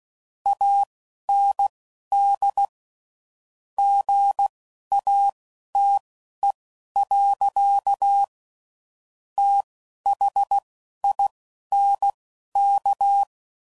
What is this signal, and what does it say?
Morse code, heard as a keyed tone, about 16 wpm, spaced as for 11 wpm, 800 Hz. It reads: AND GATE. THINK
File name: CW Code Groups @10WPM / Technical Subjects / OR Gate